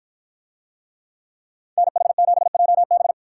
Morse code, heard as a keyed tone, about 53 wpm, 700 Hz. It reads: NH7JB